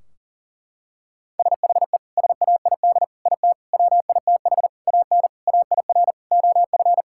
Morse code, heard as a keyed tone, about 40 wpm, 700 Hz. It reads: SHE SAID IT WITH AN AIR OF
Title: Sentences from Top 1600 Words 40wpm